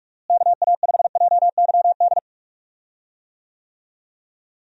Morse code, heard as a keyed tone, about 46 wpm, 700 Hz. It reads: KA5JYD
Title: USA Call Signs - 6 Characters - Repeated 2x - 45wpm